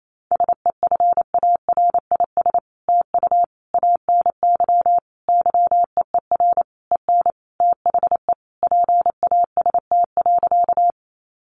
Morse code, heard as a keyed tone, about 28 wpm, 700 Hz. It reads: SEFARIS TU ANY YEER EN THE PAST.
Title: A Sound Of Thunder 28wpm